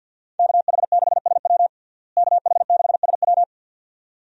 Morse code, heard as a keyed tone, about 51 wpm, 700 Hz. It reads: KH6SW KH6SW
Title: DXCC-100 Call Signs - 5 Characters - Repeated 2x - 50wpm